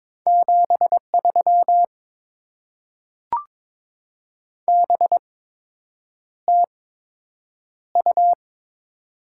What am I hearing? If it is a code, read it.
Morse code, heard as a keyed tone, about 22 wpm, spaced as for 8 wpm, 700 Hz. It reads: 73 BTU
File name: Top 20 QSO Elements - 8x Character Spacing - 22wpm